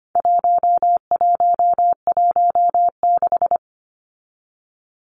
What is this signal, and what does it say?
Morse code, heard as a keyed tone, about 25 wpm, 700 Hz. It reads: 1116
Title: Numbers - 4-Digit 25wpm